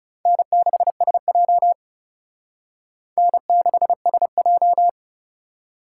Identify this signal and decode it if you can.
Morse code, heard as a keyed tone, about 39 wpm, 700 Hz. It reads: N6SJ N6SJ